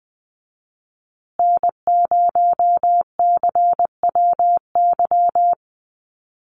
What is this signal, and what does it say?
Morse code, heard as a keyed tone, about 20 wpm, 700 Hz. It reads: N0CWY
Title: USA Call Signs Repeated 3x 20wpm